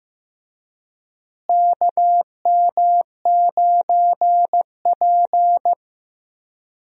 Morse code, heard as a keyed tone, about 15 wpm, 700 Hz. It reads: KM9P
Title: USA Call Signs - 4 Characters - Repeated 2x - 15wpm